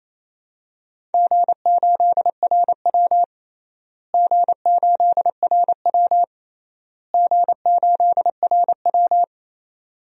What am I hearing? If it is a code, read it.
Morse code, heard as a keyed tone, about 28 wpm, 700 Hz. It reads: G8RW G8RW G8RW
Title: DXCC-100 Call Signs - 4 Characters - Repeated 3x - 28wpm